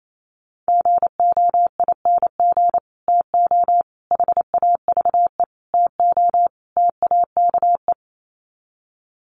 Morse code, heard as a keyed tone, about 28 wpm, 700 Hz. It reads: GOING TO HAVE TO TAKE
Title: Sets of 5 Words Encore 28wpm